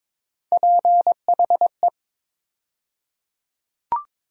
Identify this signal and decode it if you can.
Morse code, heard as a keyed tone, about 22 wpm, 700 Hz. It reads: PHE
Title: USA Call Sign Postfixes - 3 Letters 22wpm